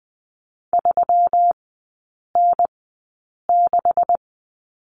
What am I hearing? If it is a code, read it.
Morse code, heard as a keyed tone, about 20 wpm, spaced as for 8 wpm, 700 Hz. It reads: 3N6
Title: 133 Lesson 22e - Question Mark - Three Characters 20wpm